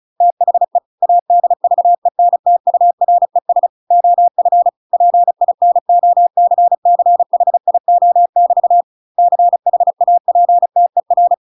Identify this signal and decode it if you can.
Morse code, heard as a keyed tone, about 34 wpm, 700 Hz. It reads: THE ADVENTURES OF PINOCCHIO<BT> CHAPTER
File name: Pinocchio / Pinocchio ch5 35 35 ik1hge